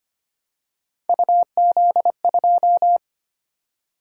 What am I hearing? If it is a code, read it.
Morse code, heard as a keyed tone, about 25 wpm, 700 Hz. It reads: UZ2
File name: Three Letter-Number Combinations - 25wpm